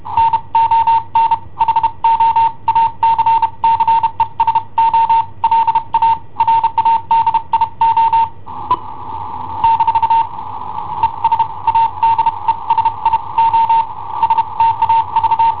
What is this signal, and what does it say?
Morse code, heard as a keyed tone, about 32 wpm, spaced as for 27 wpm, 910 Hz. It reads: NON HO ACCESO LA RADIO - ES ADESI O STAV
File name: i7ohp / cassetta i7ohp B